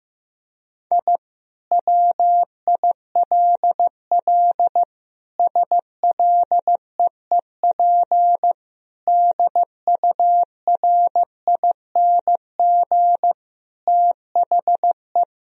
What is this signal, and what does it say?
Morse code, heard as a keyed tone, about 15 wpm, 700 Hz. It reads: I WILL SLEEP DURING THE
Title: Sentences from Top 1000 Words - Review Part 2 - 15wpm